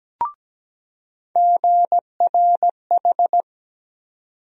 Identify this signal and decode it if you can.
Morse code, heard as a keyed tone, about 17 wpm, 700 Hz. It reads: GRH
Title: USA Call Sign Postfixes - 3 Letters 17wpm